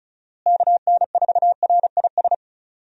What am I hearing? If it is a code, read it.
Morse code, heard as a keyed tone, about 36 wpm, 700 Hz. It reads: KN4RIS
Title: USA Call Signs - 6 Characters - Repeated 3x - 35wpm